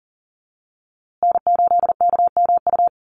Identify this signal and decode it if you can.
Morse code, heard as a keyed tone, about 38 wpm, 700 Hz. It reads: N8KMU
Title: USA Call Signs Repeated 2x 40wpm